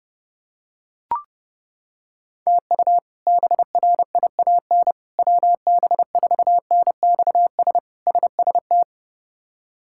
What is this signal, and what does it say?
Morse code, heard as a keyed tone, about 30 wpm, 700 Hz. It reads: TU BRIAN WB4NXS SST
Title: K1USN SST QSO Practice - 30wpm